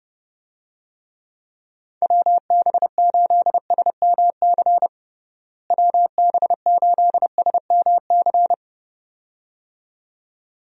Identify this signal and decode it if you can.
Morse code, heard as a keyed tone, about 30 wpm, 700 Hz. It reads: WB8SMC WB8SMC
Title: USA Call Signs - 6 Characters - Repeated 2x - 30wpm